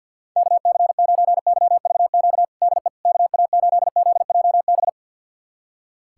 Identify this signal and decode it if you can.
Morse code, heard as a keyed tone, about 51 wpm, 700 Hz. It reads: KC9YVQ DE KA8ZJB
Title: USA Callsign DE Callsign - 6 Characters - 50wpm